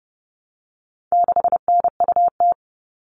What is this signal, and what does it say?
Morse code, heard as a keyed tone, about 30 wpm, 700 Hz. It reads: 6NUT
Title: All Cut Numbers - 4-Digit 30wpm